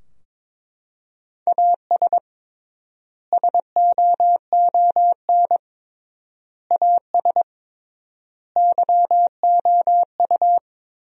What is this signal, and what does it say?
Morse code, heard as a keyed tone, about 22 wpm, 700 Hz. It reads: AS SOON AS YOU
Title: 3x Word-Spacing - Sentences from Top 400 Words - 22wpm